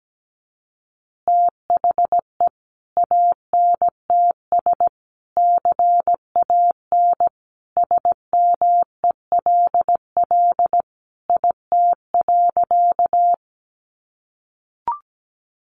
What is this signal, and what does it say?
Morse code, heard as a keyed tone, about 17 wpm, 700 Hz. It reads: THE ANTS CAN SMELL IT.
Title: Easy Phrases 17wpm